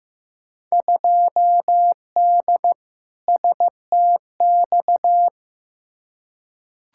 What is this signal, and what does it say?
Morse code, heard as a keyed tone, about 15 wpm, 700 Hz. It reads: 2D STX
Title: Field Day Exchanges - 15wpm